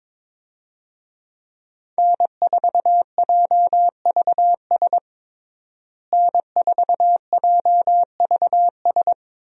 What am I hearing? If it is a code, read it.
Morse code, heard as a keyed tone, about 22 wpm, 700 Hz. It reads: N4JVS N4JVS
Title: USA Call Signs - 5 Characters - Repeated 3x - 22wpm